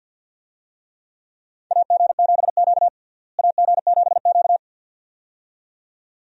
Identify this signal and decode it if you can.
Morse code, heard as a keyed tone, about 46 wpm, 700 Hz. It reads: AG7Q AG7Q
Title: USA Call Signs - 4 Characters - Repeated 2x - 50wpm